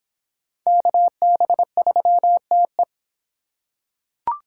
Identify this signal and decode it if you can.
Morse code, heard as a keyed tone, about 25 wpm, 700 Hz. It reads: KB3TE